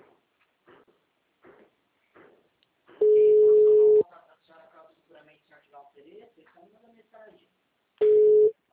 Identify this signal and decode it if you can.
Ringback tone (Europe, 425 Hz), with speech.